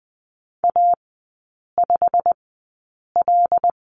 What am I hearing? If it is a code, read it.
Morse code, heard as a keyed tone, about 20 wpm, spaced as for 8 wpm, 700 Hz. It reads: A5L
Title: 087 Lesson 15e - Number 5 - Three Characters 20wpm